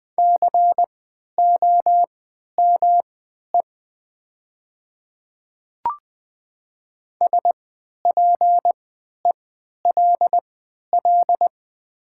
Morse code, heard as a keyed tone, about 20 wpm, spaced as for 11 wpm, 700 Hz. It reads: COME SPELL
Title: 3x Character Spacing - Top 300 Words - 20wpm